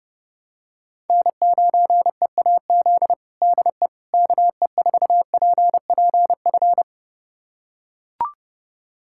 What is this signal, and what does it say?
Morse code, heard as a keyed tone, about 31 wpm, 700 Hz. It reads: N9EAZ DE KE4PPF
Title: USA Callsign DE Callsign 30wpm